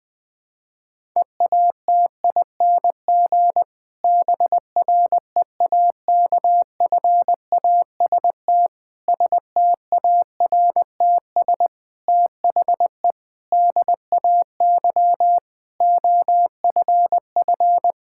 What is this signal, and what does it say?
Morse code, heard as a keyed tone, about 21 wpm, 700 Hz. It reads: EATING BREAKFAST STARTS THE DAY OFF
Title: Difficult Sentences with Hints - 20wpm